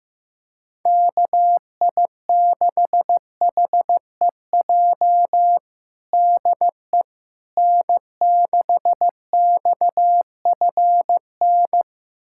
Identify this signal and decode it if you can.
Morse code, heard as a keyed tone, about 15 wpm, 700 Hz. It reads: KI6HEJ DE N6XFN
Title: USA Callsign DE Callsign 15wpm